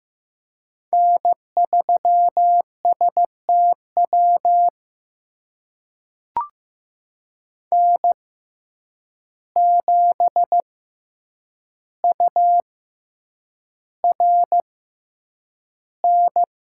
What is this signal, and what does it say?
Morse code, heard as a keyed tone, about 15 wpm, spaced as for 8 wpm, 700 Hz. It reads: N3STW N7URN
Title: USA Call Signs - 6x Character Spacing - 15wpm